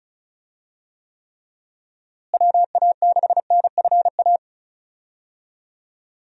Morse code, heard as a keyed tone, about 36 wpm, 700 Hz. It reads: WA6NFA
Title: USA Call Signs - 6 Characters - Repeated 2x - 35wpm